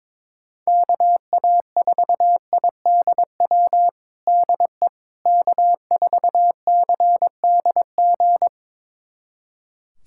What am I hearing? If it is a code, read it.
Morse code, heard as a keyed tone, about 21 wpm, 700 Hz. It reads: KA4IDW DE K4CDG
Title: USA Callsign DE Callsign 22wpm